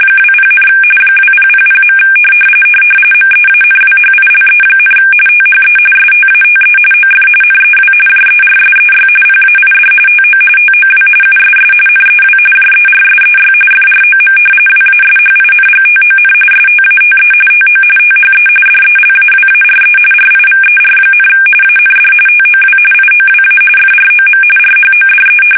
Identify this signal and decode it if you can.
RTTY, about 45.45 baud. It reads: 20, 2000. MEANWHILE, THE STATE OF NEW JERSEY PAID TO TOW USS NEW JERSEY FROM WASHINGTON STATE TO THE DELAWARE RIVER, AND TOW IOWA FROM RHODE
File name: fsk850 / battleshipnj